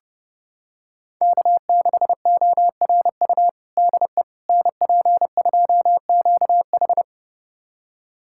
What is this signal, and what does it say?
Morse code, heard as a keyed tone, about 32 wpm, 700 Hz. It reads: K6ORU DE NP2QH
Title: USA Callsign DE Callsign - 5 Characters - 30wpm